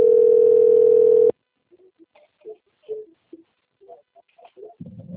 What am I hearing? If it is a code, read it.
Ringback tone (North America), 440 + 480 Hz.
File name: rg-1600-unknown- - - .14015